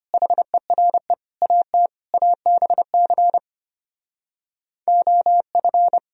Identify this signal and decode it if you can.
Morse code, heard as a keyed tone, about 33 wpm, 700 Hz. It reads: HERE AT ABC OF